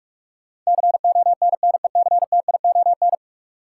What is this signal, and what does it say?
Morse code, heard as a keyed tone, about 46 wpm, 700 Hz. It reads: CONNECTION
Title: Sentences from Top 1300 Words - New Word Drill Alt - 45wpm